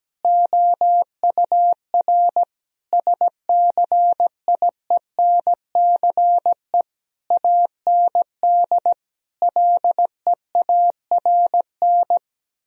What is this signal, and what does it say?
Morse code, heard as a keyed tone, about 17 wpm, 700 Hz. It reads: OUR SCIENCE AND LEARN
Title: Sentences from Top 1000 Words - Review Part 4 - 17wpm